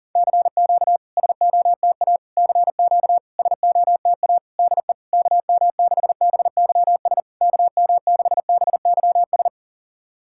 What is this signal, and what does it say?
Morse code, heard as a keyed tone, about 40 wpm, 700 Hz. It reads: CQ SOTA CQ SOTA DE KM6BYS KM6BYS
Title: SOTA QSO Practice 40wpm